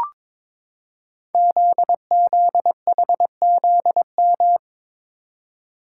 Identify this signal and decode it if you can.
Morse code, heard as a keyed tone, about 22 wpm, 700 Hz. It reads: ZZHZM
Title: Code Groups - 5 Letters - 22wpm